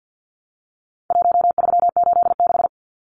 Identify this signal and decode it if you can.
Morse code, heard as a keyed tone, about 50 wpm, 700 Hz. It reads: 1386
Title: Numbers - Contest Style - 4-Digit 50wpm